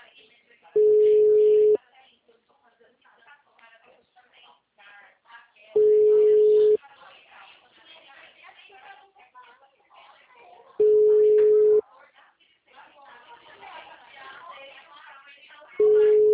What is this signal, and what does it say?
Ringback tone (Europe, 425 Hz), with speech.
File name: out- -2022- - - .82821